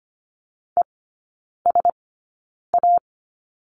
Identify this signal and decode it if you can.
Morse code, heard as a keyed tone, about 25 wpm, spaced as for 9 wpm, 700 Hz. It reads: ESA
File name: course / 25:10wpm / 093 Lesson 16e - Letter c - Three Characters 25wpm